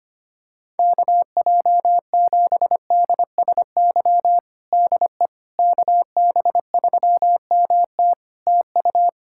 Morse code, heard as a keyed tone, about 25 wpm, 700 Hz. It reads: KJ7DSY DE KB3MT TU
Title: K1USN SST QSO Practice 25wpm